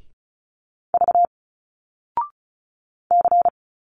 Morse code, heard as a keyed tone, about 34 wpm, 700 Hz. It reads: V C
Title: Single Letters 35wpm